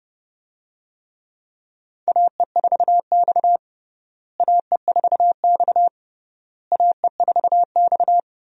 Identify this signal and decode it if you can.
Morse code, heard as a keyed tone, about 29 wpm, 700 Hz. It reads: AE4X AE4X AE4X
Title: USA Call Signs - 4 Characters - Repeated 3x - 30wpm